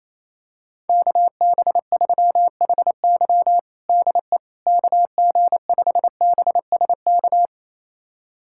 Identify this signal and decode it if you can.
Morse code, heard as a keyed tone, about 28 wpm, 700 Hz. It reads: KB3HY DE KG5BSK
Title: USA Callsign DE Callsign 28wpm